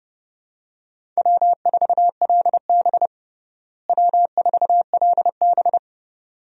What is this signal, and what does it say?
Morse code, heard as a keyed tone, about 30 wpm, 700 Hz. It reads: W4LB W4LB
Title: USA Call Signs - 4 Characters - Repeated 2x - 30wpm